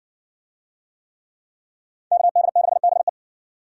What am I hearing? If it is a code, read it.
Morse code, heard as a keyed tone, about 60 wpm, 700 Hz. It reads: KD6BE